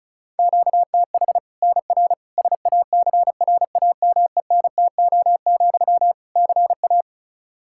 Morse code, heard as a keyed tone, about 36 wpm, 700 Hz. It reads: QTH NR SACRAMENTO, CA
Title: Practice QSO - Exchange 1 - Beginning - 35wpm